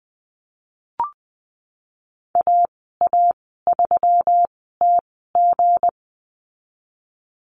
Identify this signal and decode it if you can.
Morse code, heard as a keyed tone, about 20 wpm, spaced as for 14 wpm, 700 Hz. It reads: AA3TG
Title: 186 Lesson 30g - Letter b - Call Signs 20wpm